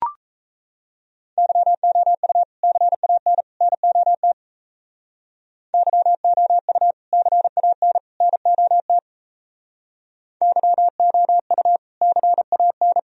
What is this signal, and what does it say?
Morse code, heard as a keyed tone, about 36 wpm, 700 Hz. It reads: YOU CAN NOT YOU CAN NOT YOU CAN